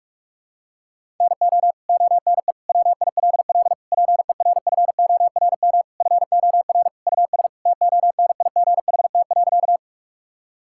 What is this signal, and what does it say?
Morse code, heard as a keyed tone, about 46 wpm, 700 Hz. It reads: NO ONE WILL PERFORM FOR US TONIGHT.
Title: Sentences from Top 1000 Words - Review Part 5 - 45wpm